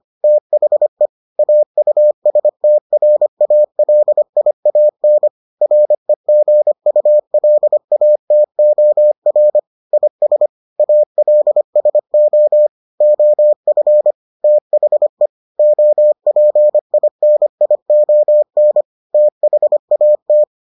Morse code, heard as a keyed tone, about 25 wpm, 600 Hz. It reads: THE AUSTRALIAN REGULATOR IS ALSO OF THE OPINION THAT